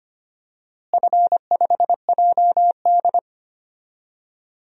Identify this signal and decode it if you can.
Morse code, heard as a keyed tone, about 25 wpm, 700 Hz. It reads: F5JD
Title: DXCC-100 Call Signs - 4 Characters - Repeated 3x - 25wpm